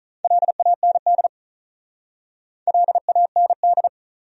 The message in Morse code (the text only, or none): LAND LAND